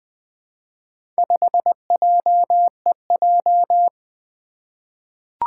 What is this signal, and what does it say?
Morse code, heard as a keyed tone, about 21 wpm, 700 Hz. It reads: 5JEJ
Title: Four Challenging Letter-Number Combinations 20wpm